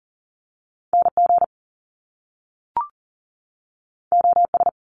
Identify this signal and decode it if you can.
Morse code, heard as a keyed tone, about 39 wpm, 700 Hz. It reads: NG OS
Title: Top Two Letter Combinations in Words 40wpm